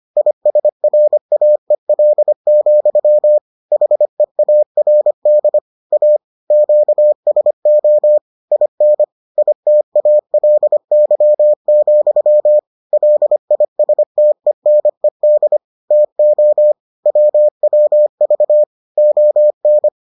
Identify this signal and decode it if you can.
Morse code, heard as a keyed tone, about 25 wpm, 600 Hz. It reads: ISRAEL, HEARD A QSO IN ITALY, LISTENED TO WWV ON